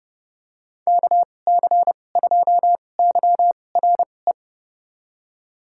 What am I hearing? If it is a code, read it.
Morse code, heard as a keyed tone, about 30 wpm, spaced as for 22 wpm, 700 Hz. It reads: KC2YRE
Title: USA Call Signs - 6 Characters - 2x Character Spacing - 30wpm